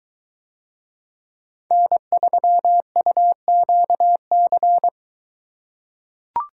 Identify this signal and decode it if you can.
Morse code, heard as a keyed tone, about 22 wpm, 700 Hz. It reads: N3UQC